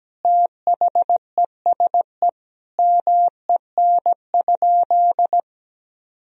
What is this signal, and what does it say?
Morse code, heard as a keyed tone, about 17 wpm, 700 Hz. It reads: THESE MEN?
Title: Sentences from Top 1000 Words - Review Part 1 - 17wpm